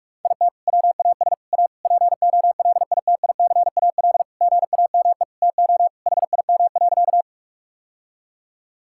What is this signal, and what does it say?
Morse code, heard as a keyed tone, about 46 wpm, 700 Hz. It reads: IT WAS A POLITICAL GAME TO HIM.
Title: Sentences from Top 1000 Words - Review Part 5 - 45wpm